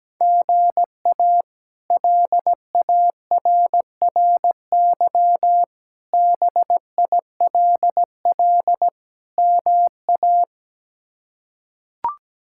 Morse code, heard as a keyed tone, about 17 wpm, 700 Hz. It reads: GA LARRY BILL MA
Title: K1USN SST QSO Practice - 17wpm